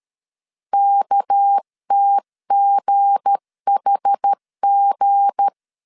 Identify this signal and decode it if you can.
Morse code, heard as a keyed tone, about 13 wpm, 800 Hz. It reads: KTGHG
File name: LEZIONI / 7 GF9Z / 7 G 12wpm 2055